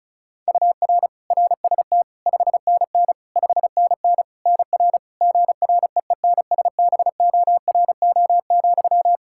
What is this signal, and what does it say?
Morse code, heard as a keyed tone, about 36 wpm, 700 Hz. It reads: UR RST 5NN 5NN NR GREENSBORO,
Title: Practice QSO - Exchange 1 - Beginning - 35wpm